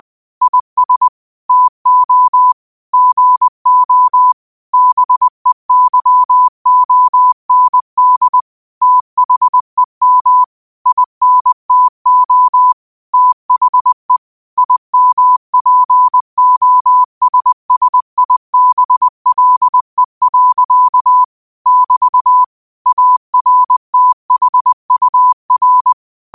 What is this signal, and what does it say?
Morse code, heard as a keyed tone, about 20 wpm, 1000 Hz. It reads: IS TO GO BEYOND THEM INTO THE IMPOSSIBLE. <BT> ARTHUR
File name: qotdfiles / 20wpm